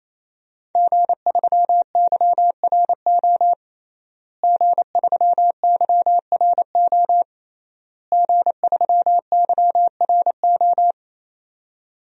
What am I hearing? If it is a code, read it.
Morse code, heard as a keyed tone, about 28 wpm, 700 Hz. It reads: G3YRO G3YRO G3YRO
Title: DXCC-100 Call Signs - 5 Characters - Repeated 3x - 28wpm